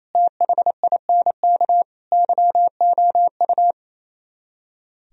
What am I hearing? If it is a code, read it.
Morse code, heard as a keyed tone, about 34 wpm, 700 Hz. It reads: THINK YOU
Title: Speed-Racing - Sets of 4 Words 28wpm